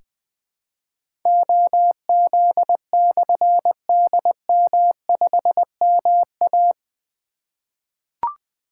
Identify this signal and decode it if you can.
Morse code, heard as a keyed tone, about 20 wpm, 700 Hz. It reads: OZ/DM5MA
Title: Difficult Call Signs 20wpm